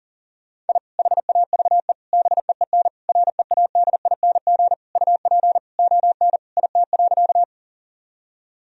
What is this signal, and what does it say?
Morse code, heard as a keyed tone, about 40 wpm, 700 Hz. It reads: I HAVE BEEN READING UP ON IT.
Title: Sentences from Top 1000 Words - Review Part 3 - 40wpm